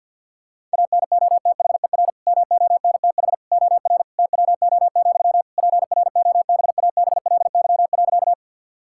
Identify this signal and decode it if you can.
Morse code, heard as a keyed tone, about 50 wpm, 700 Hz. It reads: ANOTHER MONTH OR TWO, PROBABLY.
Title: Sentences from Top 1000 Words - Review Part 5 - 50wpm